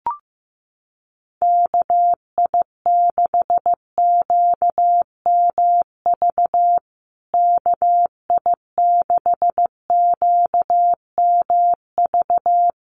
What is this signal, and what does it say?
Morse code, heard as a keyed tone, about 15 wpm, 700 Hz. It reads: KI6QMV KI6QMV
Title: USA Call Signs Repeated 3x 15wpm